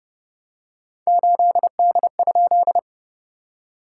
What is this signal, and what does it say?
Morse code, heard as a keyed tone, about 30 wpm, 660 Hz. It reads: 8D?
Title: SOTA QSO Practice 30wpm